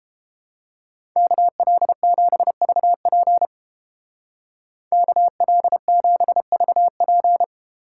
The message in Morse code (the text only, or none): KL7VP KL7VP